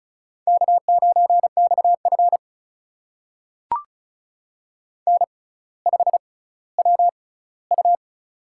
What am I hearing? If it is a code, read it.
Morse code, heard as a keyed tone, about 34 wpm, spaced as for 18 wpm, 700 Hz. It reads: K9XF N5WU
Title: USA Call Signs - 4 Characters - 6x Character Spacing - 35wpm